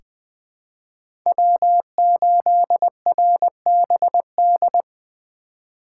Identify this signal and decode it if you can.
Morse code, heard as a keyed tone, about 20 wpm, 700 Hz. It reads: W8RBD
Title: USA Call Signs - 5 Characters - 4x Character Spacing - 20wpm